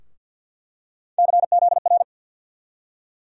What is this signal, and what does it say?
Morse code, heard as a keyed tone, about 50 wpm, 700 Hz. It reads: CZR
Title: USA Call Sign Postfixes - 3 Letters 50wpm